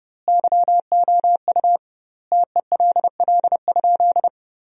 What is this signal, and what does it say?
Morse code, heard as a keyed tone, about 32 wpm, 700 Hz. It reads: YOU TELL?
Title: 2x Word-Spacing - Sentences from Top 300 Words - 30wpm